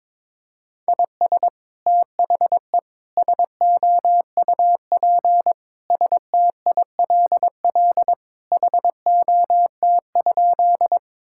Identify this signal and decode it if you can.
Morse code, heard as a keyed tone, about 23 wpm, 700 Hz. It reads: IS THE SOUP STILL HOT?
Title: Sentences from Top 1000 Words - Review Part 6 - 22wpm